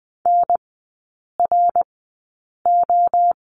Morse code, heard as a keyed tone, about 20 wpm, spaced as for 8 wpm, 700 Hz. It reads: NRO
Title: 074 Lesson 13e - Letter L - Three Characters 20wpm